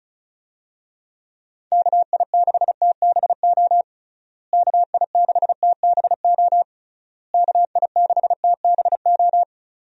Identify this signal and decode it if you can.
Morse code, heard as a keyed tone, about 36 wpm, 700 Hz. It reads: KI6TBO KI6TBO KI6TBO
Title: USA Call Signs - 6 Characters - Repeated 3x - 35wpm